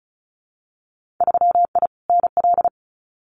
Morse code, heard as a keyed tone, about 34 wpm, 700 Hz. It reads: 3I NL
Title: Winter Field Day Exchanges 35wpm